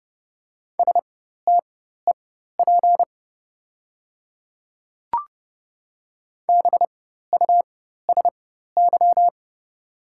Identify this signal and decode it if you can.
Morse code, heard as a keyed tone, about 30 wpm, spaced as for 14 wpm, 700 Hz. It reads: STEP BUSY
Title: 4x Character Spacing - Top 500 Words - 30wpm